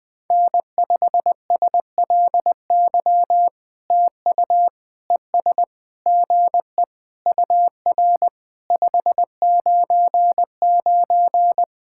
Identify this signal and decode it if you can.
Morse code, heard as a keyed tone, about 20 wpm, 700 Hz. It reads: N5SLY TU ES GE UR 599
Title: POTA QSO Practice - 20wpm